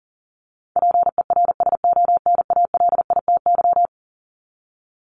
Morse code, heard as a keyed tone, about 40 wpm, 700 Hz. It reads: PERSONALITY
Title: Top 1251-1500 Words 40wpm